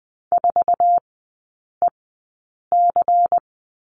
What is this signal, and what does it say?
Morse code, heard as a keyed tone, about 20 wpm, spaced as for 8 wpm, 700 Hz. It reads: 4EC
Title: 093 Lesson 16e - Letter c - Three Characters 20wpm